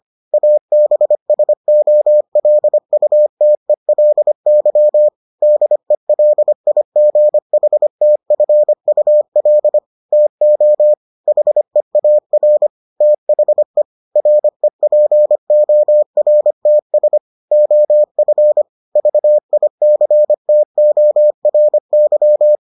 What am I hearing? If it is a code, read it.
Morse code, heard as a keyed tone, about 25 wpm, 600 Hz. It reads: ABSOLUTELY DELIGHTFUL TO HEAR THE REPORTS OF VICTORY